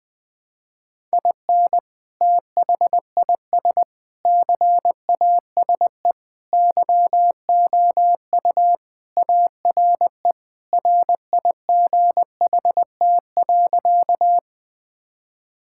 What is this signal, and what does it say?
Morse code, heard as a keyed tone, about 20 wpm, 700 Hz. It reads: IN THIS CASE YOU ARE RIGHT.
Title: Sentences with 3-Word Phrases - 20wpm